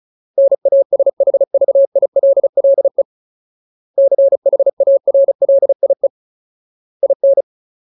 Morse code, heard as a keyed tone, about 34 wpm, 550 Hz. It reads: NASHVILLE CHARLIE IN
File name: Ultimate Word List / 35WPM Ultimate